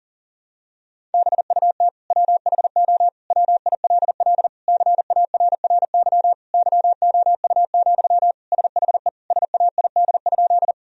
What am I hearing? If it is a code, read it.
Morse code, heard as a keyed tone, about 40 wpm, 700 Hz. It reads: BUT WHO WILL CARRY YOU, SHE SAID?
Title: Sentences from Top 1000 Words - Review Part 1 - 40wpm